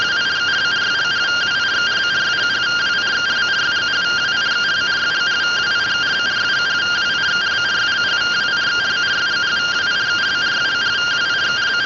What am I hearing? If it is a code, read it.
Two-tone teleprinter signal, about 170 Hz shift.